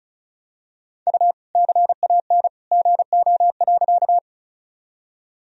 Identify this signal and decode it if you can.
Morse code, heard as a keyed tone, about 36 wpm, 700 Hz. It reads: U CAN GO.
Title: Sentences from Top 100 Words and Abbreviations 35wpm